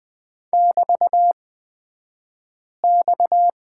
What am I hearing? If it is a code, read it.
Morse code, heard as a keyed tone, about 20 wpm, spaced as for 5 wpm, 700 Hz. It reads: <BT>X
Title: 249 Lesson 40d - BK - Two Characters 20wpm